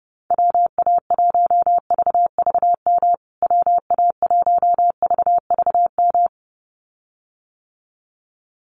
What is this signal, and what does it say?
Morse code, heard as a keyed tone, about 30 wpm, 700 Hz. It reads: WA1VVM WA1VVM
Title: USA Call Signs Repeated 2x 30wpm